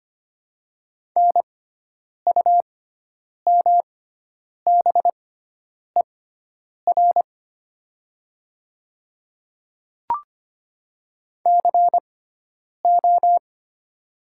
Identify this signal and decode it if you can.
Morse code, heard as a keyed tone, about 25 wpm, spaced as for 9 wpm, 700 Hz. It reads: NUMBER CO
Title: 6x Character Spacing - Top 100 Words - 25wpm